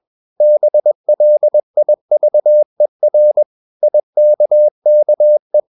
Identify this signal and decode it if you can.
Morse code, heard as a keyed tone, about 21 wpm, 600 Hz. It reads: BLIVER IKKE
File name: Ferie QRS 21wpm